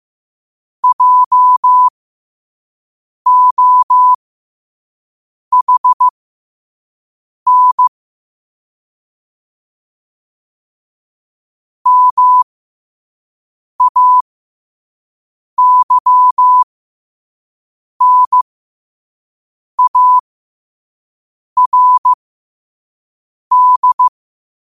Morse code, heard as a keyed tone, about 15 wpm, spaced as for 5 wpm, 1000 Hz. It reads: JOHN MAYNARD
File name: qotdfiles / 5wpm